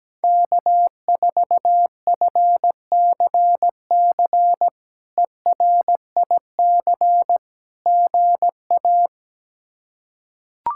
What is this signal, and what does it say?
Morse code, heard as a keyed tone, about 17 wpm, 700 Hz. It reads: K4FCC ERIC GA
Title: K1USN SST QSO Practice - 17wpm